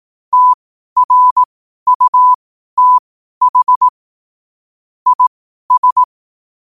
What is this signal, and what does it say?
Morse code, heard as a keyed tone, about 18 wpm, spaced as for 12 wpm, 1000 Hz. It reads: TRUTH IS
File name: qotdfiles / 12wpm